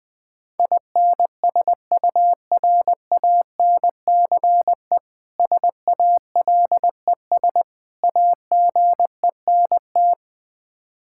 Morse code, heard as a keyed tone, about 20 wpm, 700 Hz. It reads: INSURANCE SALES AGENT
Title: Practice QSO - Exchange 3 - More Info - 20wpm